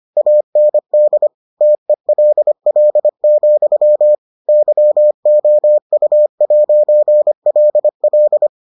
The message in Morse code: AND TELL, YOU'LL

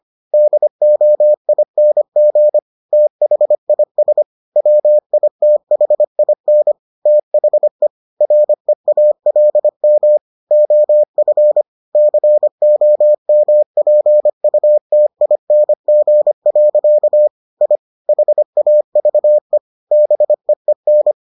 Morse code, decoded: DOING THIS WITHIN THE REALM OF COMPUTING. I HAVE BEEN

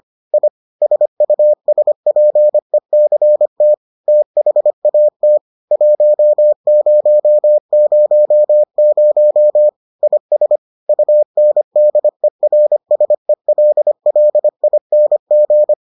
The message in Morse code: I SUSPECT THAT 1000 IS UNDERSELLING

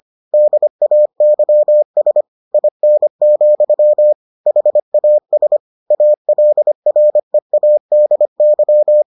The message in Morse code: DAYS IN, HAS ALREADY

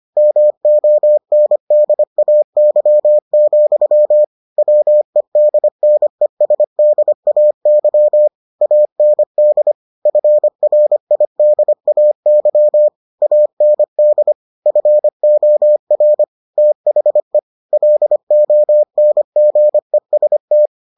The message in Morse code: MONDAY, WEDNESDAY AND FRIDAY AND FOR THE LONGEST

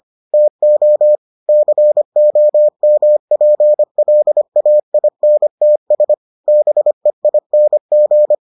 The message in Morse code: TO COMPLAINTS BEING

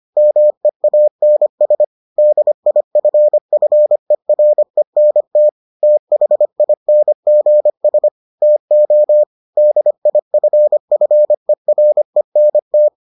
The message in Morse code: MEANS DIFFERENT THINGS TO DIFFERENT

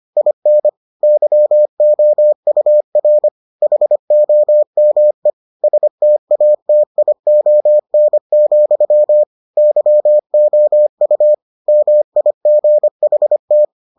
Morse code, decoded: IN YOUR HOME STATION, YOU MIGHT

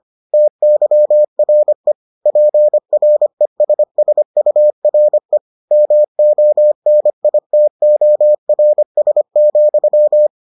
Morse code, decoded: TYRE PRESSURE MONITORS,

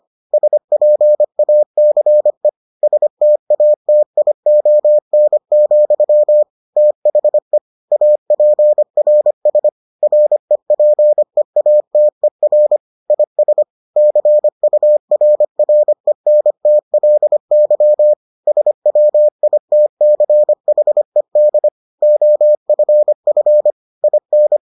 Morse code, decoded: SPACE STATION, THE APRS REPEATER IS CURRENTLY SWITCHED OFF IN

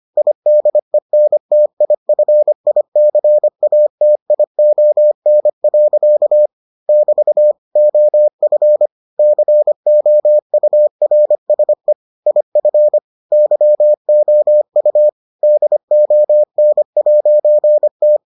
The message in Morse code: IDENTIFICATION. <BT> OF COURSE IF YOU DON'T

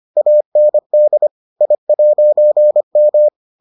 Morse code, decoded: AND I'M

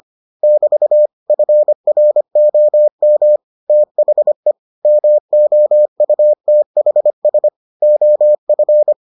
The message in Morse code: <BT> FROM THE MOUTHS OF

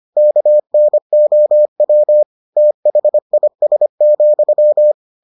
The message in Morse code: KNOW THIS,

